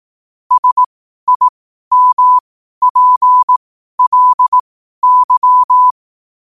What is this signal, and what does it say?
Morse code, heard as a keyed tone, about 18 wpm, spaced as for 12 wpm, 1000 Hz. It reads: SIMPLY
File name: qotdfiles / 12wpm